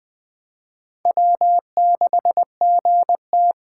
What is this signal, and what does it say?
Morse code, heard as a keyed tone, about 21 wpm, 700 Hz. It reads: W6GT
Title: Speed Racing - USA Call Signs 17wpm